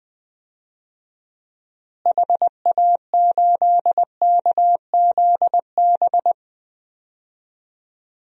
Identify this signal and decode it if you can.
Morse code, heard as a keyed tone, about 20 wpm, 700 Hz. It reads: HA8KZB
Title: DXCC-100 Call Signs - 6 Characters - Repeated 3x - 20wpm